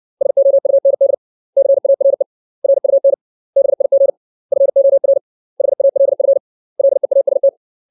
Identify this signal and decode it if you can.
Morse code, heard as a keyed tone, about 60 wpm, 550 Hz. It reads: SOUND CARE RUN BIG FOR HALF LEAST
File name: 632 words / 632 Words 60WPM